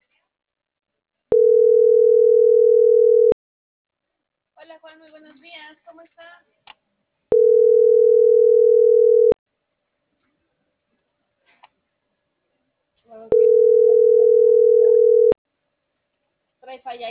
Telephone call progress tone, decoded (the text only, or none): ringback tone